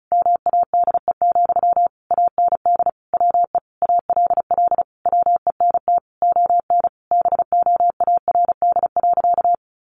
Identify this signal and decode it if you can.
Morse code, heard as a keyed tone, about 34 wpm, 700 Hz. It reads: MADE, AND WE ALL WENT ON BOARD.
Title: Sentences from Top 1400 Words 35wpm